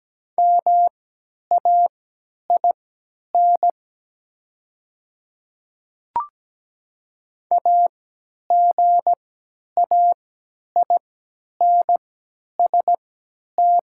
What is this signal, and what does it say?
Morse code, heard as a keyed tone, about 17 wpm, spaced as for 10 wpm, 700 Hz. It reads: MAIN AGAINST
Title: 3x Character Spacing - Top 500 Words - 17wpm